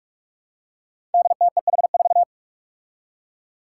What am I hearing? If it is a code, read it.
Morse code, heard as a keyed tone, about 44 wpm, 700 Hz. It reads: DTEH4
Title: Five Challenging Letter-Number Combinations 45wpm